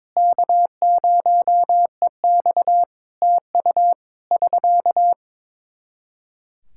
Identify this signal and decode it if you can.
Morse code, heard as a keyed tone, about 22 wpm, 700 Hz. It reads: K0EX TU <SK>
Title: Practice QSO - Exchange 4 - Ending - 22wpm